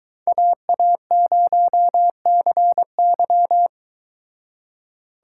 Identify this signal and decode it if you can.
Morse code, heard as a keyed tone, about 22 wpm, 700 Hz. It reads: AA0CY